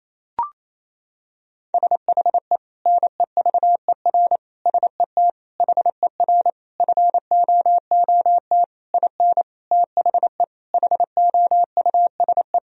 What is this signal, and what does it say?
Morse code, heard as a keyed tone, about 28 wpm, 700 Hz. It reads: SHE NEVER SET HER FOOT IN THE HOUSE
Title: Sentences from Top 1000 Words - Review Part 3 - 28wpm